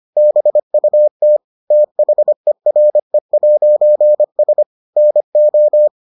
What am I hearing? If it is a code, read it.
Morse code, heard as a keyed tone, about 25 wpm, 600 Hz. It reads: BUT THERE'S NO